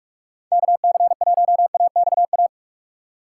Morse code, heard as a keyed tone, about 42 wpm, 700 Hz. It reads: KC1AXA